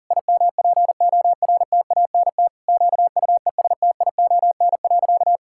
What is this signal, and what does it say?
Morse code, heard as a keyed tone, about 37 wpm, 700 Hz. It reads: IMPORTANT QUESTION.
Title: Sentences from Top 1000 Words - Review Part 6 - 40wpm